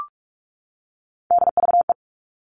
Morse code, heard as a keyed tone, about 44 wpm, 700 Hz. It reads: DVE
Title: All Cut Numbers - 3-Digit 45wpm